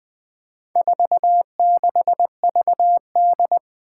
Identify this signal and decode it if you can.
Morse code, heard as a keyed tone, about 20 wpm, 700 Hz. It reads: 46VD
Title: Four Challenging Letter-Number Combinations 20wpm